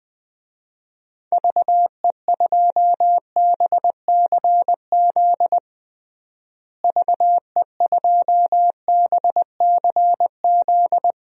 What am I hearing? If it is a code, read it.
Morse code, heard as a keyed tone, about 19 wpm, 700 Hz. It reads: VE2BCZ VE2BCZ
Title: DXCC-100 Call Signs - 6 Characters - Repeated 2x - 20wpm